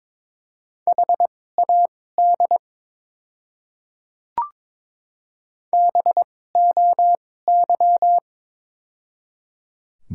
Morse code, heard as a keyed tone, about 22 wpm, spaced as for 16 wpm, 700 Hz. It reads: HAD BOY
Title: 2x Character Spacing - Top 500 Words - 22wpm